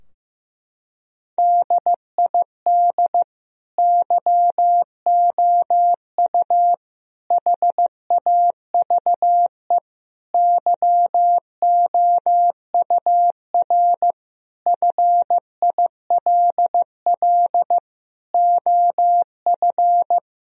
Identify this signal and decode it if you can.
Morse code, heard as a keyed tone, about 15 wpm, 700 Hz. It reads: DID YOU HAVE YOUR FILL OF
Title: Sentences from Top 1000 Words - Review Part 3 - 15wpm